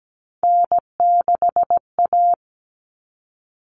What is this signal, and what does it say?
Morse code, heard as a keyed tone, about 17 wpm, 700 Hz. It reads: N6A
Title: All Cut Numbers - 3-Digit 17wpm